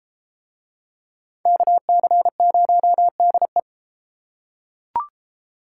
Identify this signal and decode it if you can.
Morse code, heard as a keyed tone, about 29 wpm, 700 Hz. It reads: KC0DE